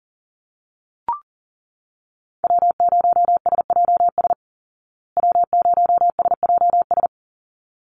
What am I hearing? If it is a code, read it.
Morse code, heard as a keyed tone, about 37 wpm, 700 Hz. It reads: W0SJS W0SJS
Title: Call Signs Repeated 3x - 40wpm